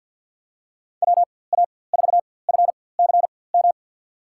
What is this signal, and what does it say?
Morse code, heard as a keyed tone, about 52 wpm, spaced as for 24 wpm, 700 Hz. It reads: WA4FCM
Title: USA Call Signs - 6 Characters - 4x Character Spacing - 50wpm